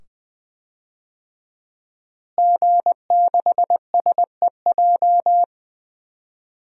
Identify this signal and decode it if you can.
Morse code, heard as a keyed tone, about 20 wpm, 700 Hz. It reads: G6SEJ
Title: DXCC-100 Call Signs - 5 Characters - Repeated 2x - 20wpm